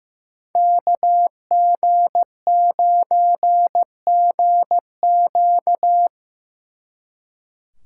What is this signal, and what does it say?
Morse code, heard as a keyed tone, about 18 wpm, 700 Hz. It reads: KG9GQ
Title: Speed Racing - USA 2x2 Call Signs 15wpm